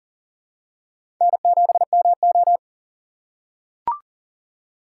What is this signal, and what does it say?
Morse code, heard as a keyed tone, about 41 wpm, 700 Hz. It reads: N7MO
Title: USA Call Signs - 3x Character Spacing - 40wpm